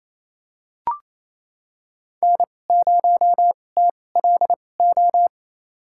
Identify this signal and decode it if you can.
Morse code, heard as a keyed tone, about 28 wpm, spaced as for 20 wpm, 700 Hz. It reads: N0TLO
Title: USA Call Signs - 5 Characters - 2x Character Spacing - 28wpm